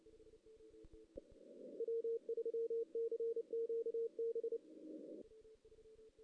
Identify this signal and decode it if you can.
Morse code, heard as a keyed tone, about 28 wpm, 460 Hz. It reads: W3CQB
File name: 2016 JAN CW NAQP / 2016 Jan naqp k5ot - 0-2